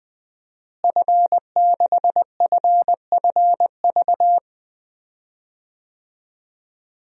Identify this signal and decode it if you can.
Morse code, heard as a keyed tone, about 20 wpm, 700 Hz. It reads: F6FFV
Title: DXCC-100 Call Signs - 5 Characters - Repeated 2x - 20wpm